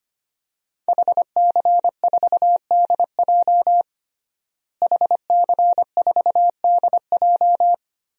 Morse code, heard as a keyed tone, about 25 wpm, 700 Hz. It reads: HC4DJ HC4DJ
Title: DXCC-100 Call Signs - 5 Characters - Repeated 3x - 25wpm